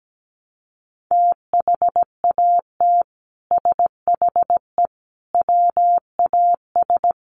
Morse code, heard as a keyed tone, about 17 wpm, 700 Hz. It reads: THAT SHE WAS
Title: Sets of 5 Words Encore2 17wpm